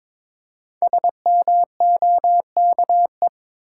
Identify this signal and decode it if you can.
Morse code, heard as a keyed tone, about 22 wpm, 700 Hz. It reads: SMOKE
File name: 5 Letter Words - First 3 Letters Given - 22wpm